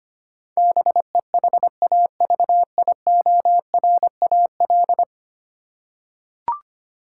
Morse code, heard as a keyed tone, about 25 wpm, 700 Hz. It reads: BEHAVIORAL
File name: Ten Letter Words - 25wpm